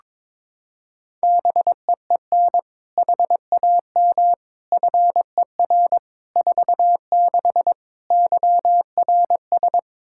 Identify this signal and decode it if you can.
Morse code, heard as a keyed tone, about 22 wpm, 700 Hz. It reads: BEEN HAM FER 46 YRS
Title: K1USN SST QSO Practice 22wpm